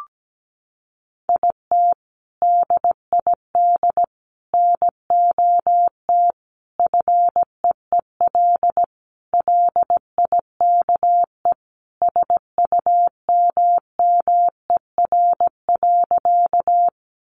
Morse code, heard as a keyed tone, about 17 wpm, 700 Hz. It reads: IT DID NOT FEEL LIKE SUMMER.
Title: Sentences from Top 1100 Words 17wpm